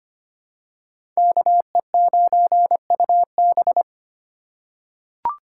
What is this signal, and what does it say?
Morse code, heard as a keyed tone, about 25 wpm, 700 Hz. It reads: KE9UB
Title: USA Call Signs - 5 Characters - 4x Character Spacing - 25wpm